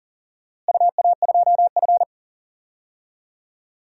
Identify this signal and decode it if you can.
Morse code, heard as a keyed tone, about 40 wpm, 700 Hz. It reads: UA2F
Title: DXCC-100 Call Signs - 4 Characters - Repeated 3x - 40wpm